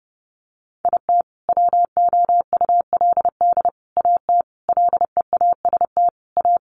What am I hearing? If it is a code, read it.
Morse code, heard as a keyed tone, about 30 wpm, 700 Hz. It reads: IT WOULD AT LEAST A
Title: Sentences from Top 700 Words 30wpm